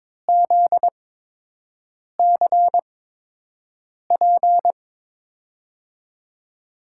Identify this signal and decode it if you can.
Morse code, heard as a keyed tone, about 22 wpm, spaced as for 6 wpm, 700 Hz. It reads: ZCP
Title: USA Call Signs - 8x Character Spacing - 22wpm